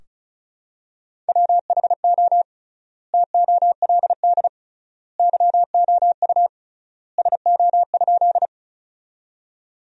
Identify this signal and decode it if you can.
Morse code, heard as a keyed tone, about 34 wpm, 700 Hz. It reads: WHO TOLD YOU SO?
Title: 3x Word-Spacing - Sentences from Top 400 Words - 35wpm